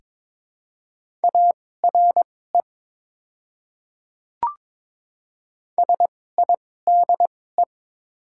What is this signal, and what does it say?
Morse code, heard as a keyed tone, about 22 wpm, spaced as for 16 wpm, 690 Hz. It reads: ARE SIDE
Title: 2x Character Spacing - Top 200 Words - 22wpm